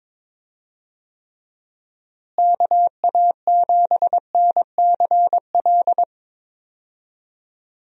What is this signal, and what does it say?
Morse code, heard as a keyed tone, about 22 wpm, 700 Hz. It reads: KA7NCL
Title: USA Call Signs - 6 Characters - Repeated 3x - 22wpm